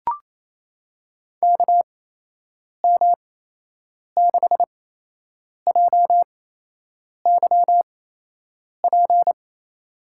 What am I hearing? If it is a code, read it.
Morse code, heard as a keyed tone, about 28 wpm, spaced as for 8 wpm, 700 Hz. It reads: KM6JYP
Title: USA Call Signs - 6 Characters - 8x Character Spacing - 28wpm